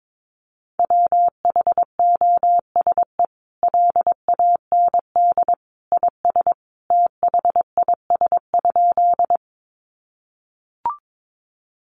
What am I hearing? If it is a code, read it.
Morse code, heard as a keyed tone, about 22 wpm, 700 Hz. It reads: WHOSE LAND IS THIS?
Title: Sentences from Top 1200 Words 22wpm